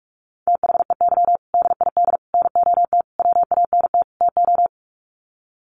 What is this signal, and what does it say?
Morse code, heard as a keyed tone, about 44 wpm, 700 Hz. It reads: THEY DID NOT WANT TO
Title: Sets of 5 Words 45wpm